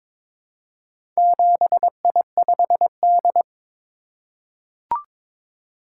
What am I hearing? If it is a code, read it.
Morse code, heard as a keyed tone, about 22 wpm, 700 Hz. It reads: 7I5D
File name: practice / 22wpm / Four Letter-Number Combinations - 22wpm